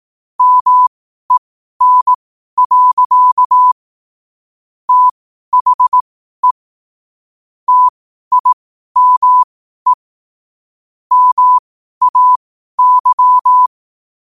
Morse code, heard as a keyed tone, about 18 wpm, spaced as for 12 wpm, 1000 Hz. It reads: MEN. THE TIME MAY
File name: qotdfiles / 12wpm